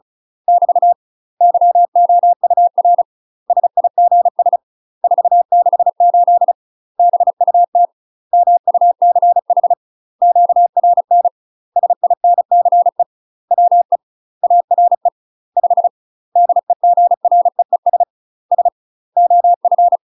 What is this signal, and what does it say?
Morse code, heard as a keyed tone, about 36 wpm, 700 Hz. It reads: <BT> YOUR SIGS 468 BUT MUCH QRN SINCE WE ARE 5 DEGREES S OF
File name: UK MASTER QSOs TRAINING / 35WPM / UK MASTER QSOs 0020